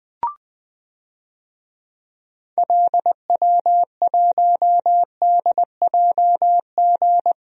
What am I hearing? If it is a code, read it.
Morse code, heard as a keyed tone, about 20 wpm, 690 Hz. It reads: LW1DJG
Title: DXCC-100 Call Signs - 6 Characters - Repeated 2x - 20wpm